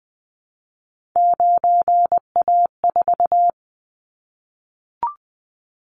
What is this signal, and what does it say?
Morse code, heard as a keyed tone, about 20 wpm, 700 Hz. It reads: 9A4
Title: Callsign Prefixes for Top 25 DXCC - 20wpm